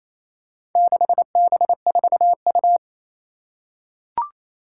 Morse code, heard as a keyed tone, about 28 wpm, 700 Hz. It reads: 6B4U
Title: Four Challenging Letter-Number Combinations 28wpm